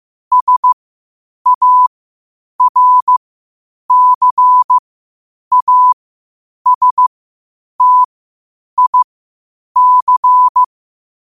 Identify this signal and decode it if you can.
Morse code, heard as a keyed tone, about 15 wpm, spaced as for 8 wpm, 1000 Hz. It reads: SARCASTIC